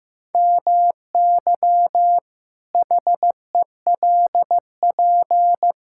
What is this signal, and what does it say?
Morse code, heard as a keyed tone, about 16 wpm, 700 Hz. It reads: MY HELP
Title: Pangram Challenge v2 - 15wpm